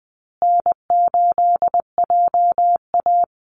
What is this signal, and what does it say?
Morse code, heard as a keyed tone, about 20 wpm, 700 Hz. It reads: N8JA
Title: Easy Call Signs Repeated 3x - 20wpm